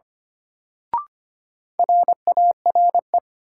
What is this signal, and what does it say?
Morse code, heard as a keyed tone, about 25 wpm, 700 Hz. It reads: RARE
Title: Sentences from Top 1100 Words - New Word Drill - 25wpm